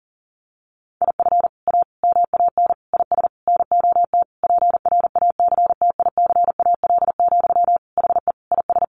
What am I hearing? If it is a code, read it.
Morse code, heard as a keyed tone, about 40 wpm, 700 Hz. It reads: IF A MAN IS NOT PRACTICAL, HE IS
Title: Sentences from Top 1400 Words 40wpm